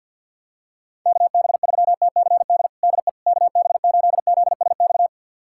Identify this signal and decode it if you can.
Morse code, heard as a keyed tone, about 47 wpm, 690 Hz. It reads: KB3TCD DE KB8ZSX
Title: USA Callsign DE Callsign - 6 Characters - 50wpm